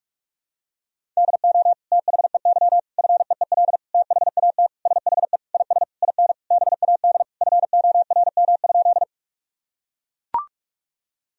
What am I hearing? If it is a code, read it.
Morse code, heard as a keyed tone, about 41 wpm, 700 Hz. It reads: DO THEY FEEL THAT SHE IS IN BAD FORM?
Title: Sentences from Top 1000 Words - Review Part 1 - 45wpm